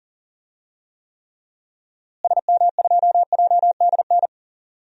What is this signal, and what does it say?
Morse code, heard as a keyed tone, about 40 wpm, 700 Hz. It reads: SM2JDN
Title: DXCC-100 Call Signs - 6 Characters - Repeated 2x - 40wpm